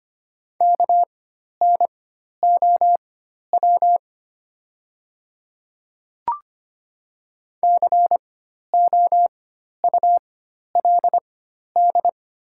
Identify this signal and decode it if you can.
Morse code, heard as a keyed tone, about 25 wpm, spaced as for 12 wpm, 700 Hz. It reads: KNOW COULD
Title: 4x Character Spacing - Top 100 Words - 25wpm